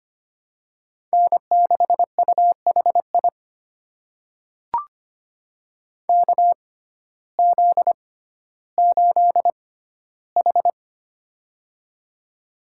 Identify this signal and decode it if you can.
Morse code, heard as a keyed tone, about 25 wpm, spaced as for 14 wpm, 700 Hz. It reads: N6UHI KZ8H
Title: USA Call Signs - 6x Character Spacing - 25wpm